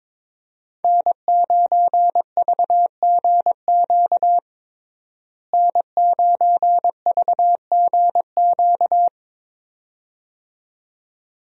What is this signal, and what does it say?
Morse code, heard as a keyed tone, about 22 wpm, 700 Hz. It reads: N9VGQ N9VGQ
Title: USA Call Signs - 5 Characters - Repeated 2x - 22wpm